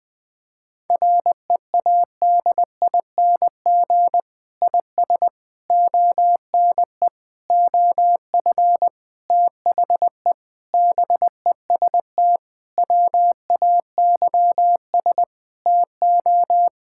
Morse code, heard as a keyed tone, about 20 wpm, 700 Hz. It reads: READING IS ONE OF THE BEST WAYS TO
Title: Sentences with 6-Word Phrases - Spoken Beginning - 20wpm